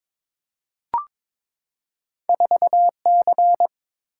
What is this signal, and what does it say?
Morse code, heard as a keyed tone, about 22 wpm, 700 Hz. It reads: 4C
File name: Two Letter-Number Combinations - 22wpm